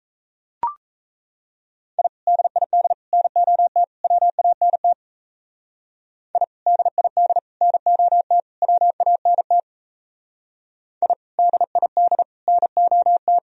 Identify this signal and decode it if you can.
Morse code, heard as a keyed tone, about 35 wpm, 700 Hz. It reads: I DID NOT WANT I DID NOT WANT I DID NOT
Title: Speed-Racing - Sets of 4 Words 28wpm